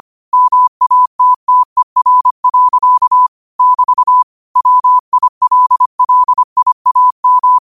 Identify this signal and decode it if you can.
Morse code, heard as a keyed tone, about 25 wpm, 1000 Hz. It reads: MATTER. <BT> WILLIAM
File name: qotdfiles / 25wpm